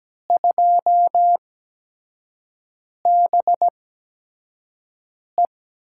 Morse code, heard as a keyed tone, about 17 wpm, spaced as for 5 wpm, 700 Hz. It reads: 2BE
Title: USA Call Signs - 5 Characters - 8x Character Spacing - 17wpm